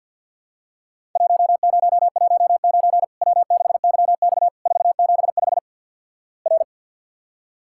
Morse code, heard as a keyed tone, about 51 wpm, 700 Hz. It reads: 1019 W6YX 475 R
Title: Worked All Europe DX Contest - QTC Exchange with Repeats - 50wpm